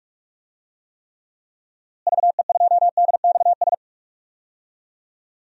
Morse code, heard as a keyed tone, about 44 wpm, 700 Hz. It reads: VE2DXS
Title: DXCC-100 Call Signs - 6 Characters - Repeated 3x - 45wpm